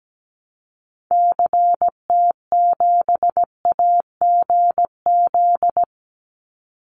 Morse code, heard as a keyed tone, about 17 wpm, 700 Hz. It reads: CT7AGZ
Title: Call Signs - 17wpm